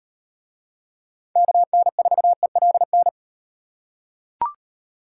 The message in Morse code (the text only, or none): KN4ELN